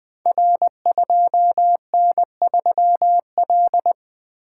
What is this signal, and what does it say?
Morse code, heard as a keyed tone, about 20 wpm, 700 Hz. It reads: R2N3L
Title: Code Groups - 5 Letter:Numbers - 20wpm